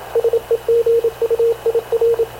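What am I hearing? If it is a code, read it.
Morse code, heard as a keyed tone, about 26 wpm, 460 Hz, with radio noise. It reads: SEGUIR
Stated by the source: Havana Radio / CLA sending Spanish press (PX) in Morse, recorded by Radio Officer Paul Dumesnil onboard CNAV QUEST/CZDO.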